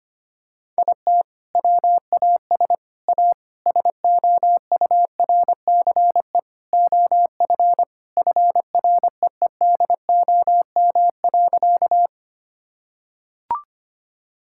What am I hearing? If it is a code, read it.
Morse code, heard as a keyed tone, about 25 wpm, 700 Hz. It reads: IT WAS A SOURCE OF FREEDOM.
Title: Sentences from Top 1000 Words - Review Part 5 - 25wpm